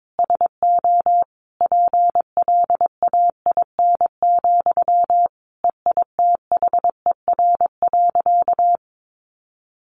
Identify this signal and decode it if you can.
Morse code, heard as a keyed tone, about 22 wpm, 700 Hz. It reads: SO PLAIN, EITHER.
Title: Sentences from Top 600 Words 22wpm